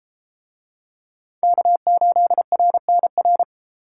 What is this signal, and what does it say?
Morse code, heard as a keyed tone, about 31 wpm, 700 Hz. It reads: K8RNR